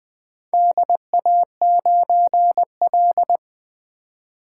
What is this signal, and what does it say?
Morse code, heard as a keyed tone, about 20 wpm, 700 Hz. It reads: DA9L
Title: Four Letter-Number Combinations - 20wpm